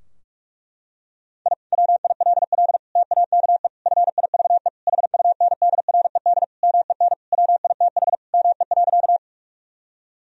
Morse code, heard as a keyed tone, about 41 wpm, 700 Hz. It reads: I WILL TAKE FIVE HUNDRED MEN WITH ME.
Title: Sentences from Top 1000 Words - Review Part 2 - 45wpm